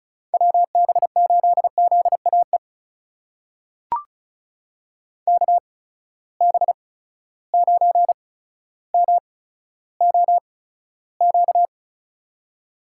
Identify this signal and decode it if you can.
Morse code, heard as a keyed tone, about 36 wpm, spaced as for 15 wpm, 700 Hz. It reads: WB8ZAE KB9MOQ
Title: USA Call Signs - 6 Characters - 8x Character Spacing - 35wpm